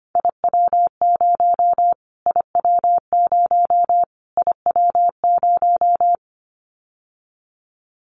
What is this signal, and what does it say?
Morse code, heard as a keyed tone, about 25 wpm, 700 Hz. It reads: IW0 IW0 IW0
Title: Callsign Prefixes for Top 25 DXCC - Repeated 3x - 25wpm